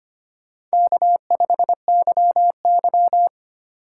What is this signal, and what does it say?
Morse code, heard as a keyed tone, about 25 wpm, 700 Hz. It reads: K5YY
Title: USA Call Signs - 4 Characters - 2x Character Spacing - 25wpm